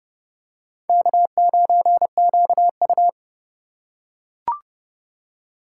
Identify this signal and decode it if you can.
Morse code, heard as a keyed tone, about 29 wpm, 700 Hz. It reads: K9QU